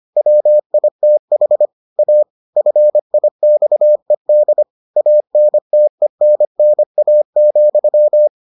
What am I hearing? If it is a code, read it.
Morse code, heard as a keyed tone, about 25 wpm, 600 Hz. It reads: WITH A FIXED ANTENNA,